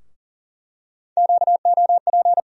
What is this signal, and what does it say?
Morse code, heard as a keyed tone, about 37 wpm, 700 Hz. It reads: QOP
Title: USA Call Sign Postfixes - 3 Letters 40wpm